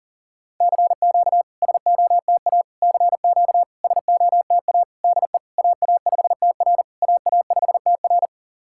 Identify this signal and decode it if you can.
Morse code, heard as a keyed tone, about 38 wpm, 690 Hz. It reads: CQ SOTA CQ SOTA DE AA5TR AA5TR
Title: SOTA QSO Practice 40wpm